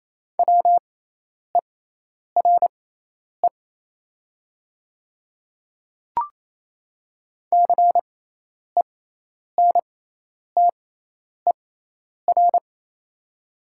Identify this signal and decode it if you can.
Morse code, heard as a keyed tone, about 28 wpm, spaced as for 10 wpm, 700 Hz. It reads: WERE CENTER
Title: 6x Character Spacing - Top 500 Words - 28wpm